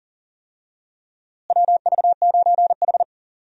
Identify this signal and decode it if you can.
Morse code, heard as a keyed tone, about 37 wpm, 700 Hz. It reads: WV9H
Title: USA Call Signs - 4 Characters - Repeated 3x - 40wpm